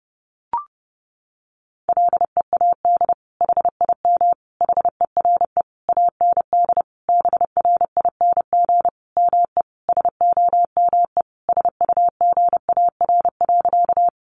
Morse code, heard as a keyed tone, about 30 wpm, 700 Hz. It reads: LEAD HIM HERE AND BRING ME SOME SUGAR.
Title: Sentences from Top 1200 Words 30wpm